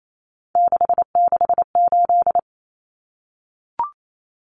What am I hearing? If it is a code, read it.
Morse code, heard as a keyed tone, about 28 wpm, 700 Hz. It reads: 668
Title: Common Cut Numbers - 3-Digit 28wpm